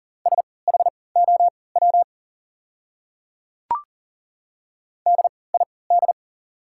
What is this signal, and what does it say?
Morse code, heard as a keyed tone, about 40 wpm, spaced as for 23 wpm, 700 Hz. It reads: SHOW DID
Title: 3x Character Spacing - Top 300 Words - 40wpm